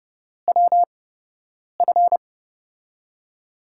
Morse code, heard as a keyed tone, about 32 wpm, spaced as for 8 wpm, 700 Hz. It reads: WF
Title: USA Call Signs - 5 Characters - 8x Character Spacing - 30wpm